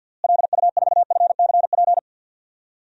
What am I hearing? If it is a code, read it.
Morse code, heard as a keyed tone, about 51 wpm, 700 Hz. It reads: LU4FCP
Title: DXCC-100 Call Signs - 6 Characters - Repeated 2x - 50wpm